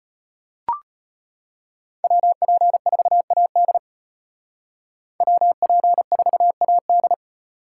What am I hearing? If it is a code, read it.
Morse code, heard as a keyed tone, about 32 wpm, 700 Hz. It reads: WP4AD WP4AD